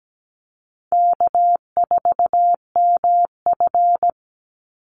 Morse code, heard as a keyed tone, about 17 wpm, 700 Hz. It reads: K4MF
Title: NA SOTA Chaser Callsigns 17wpm